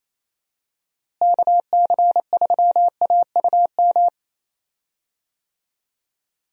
Morse code, heard as a keyed tone, about 28 wpm, 700 Hz. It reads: KC3AUM
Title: USA Call Signs - 6 Characters - Repeated 2x - 28wpm